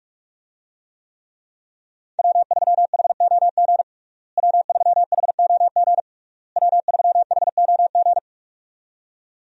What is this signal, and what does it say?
Morse code, heard as a keyed tone, about 46 wpm, 700 Hz. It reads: W3HOG W3HOG W3HOG
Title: USA Call Signs - 5 Characters - Repeated 3x - 45wpm